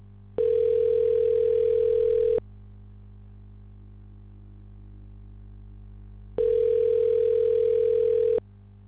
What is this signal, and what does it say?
Ringback tone (North America), 440 + 480 Hz.